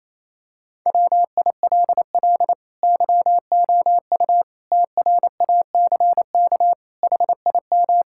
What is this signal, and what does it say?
Morse code, heard as a keyed tone, about 28 wpm, 700 Hz. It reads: WILL YOU TRACK HIM
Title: Sentences from Top 1600 Words 28wpm